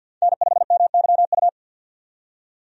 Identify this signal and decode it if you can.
Morse code, heard as a keyed tone, about 47 wpm, 700 Hz. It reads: N5MYU
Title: USA Call Signs - 5 Characters - 50wpm